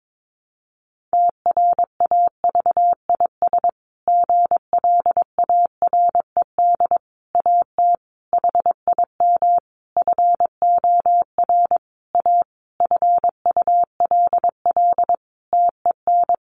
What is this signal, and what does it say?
Morse code, heard as a keyed tone, about 22 wpm, 700 Hz. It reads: TRAVIS GLARED AT HIM FOR A FULL TEN
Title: A Sound Of Thunder 22wpm